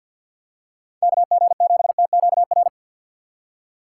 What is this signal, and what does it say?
Morse code, heard as a keyed tone, about 53 wpm, 700 Hz. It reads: KG7TQR
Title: USA Call Signs - 6 Characters - 3x Character Spacing - 50wpm